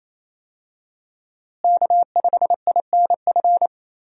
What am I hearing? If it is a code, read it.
Morse code, heard as a keyed tone, about 28 wpm, 700 Hz. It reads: K5INF
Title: USA Call Signs - 5 Characters - Repeated 3x - 28wpm